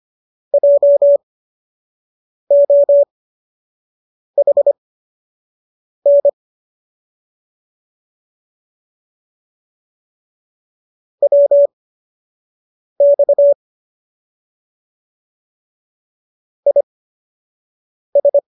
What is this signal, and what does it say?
Morse code, heard as a keyed tone, about 25 wpm, spaced as for 6 wpm, 570 Hz. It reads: JOHN WX IS